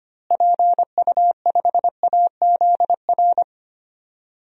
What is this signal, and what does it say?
Morse code, heard as a keyed tone, about 25 wpm, 700 Hz. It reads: PU5AZR
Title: DXCC-100 Call Signs - 6 Characters - Repeated 3x - 25wpm